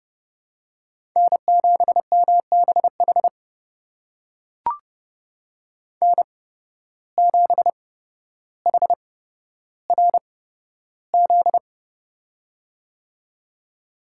Morse code, heard as a keyed tone, about 30 wpm, spaced as for 13 wpm, 700 Hz. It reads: N7MBH N7HRZ
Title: USA Call Signs - 5 Characters - 8x Character Spacing - 30wpm